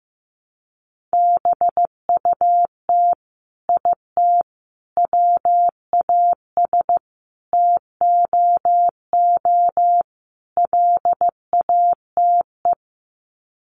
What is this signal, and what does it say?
Morse code, heard as a keyed tone, about 15 wpm, 700 Hz. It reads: BUT IT WAS TOO LATE
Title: Sets of 5 Words Encore2 15wpm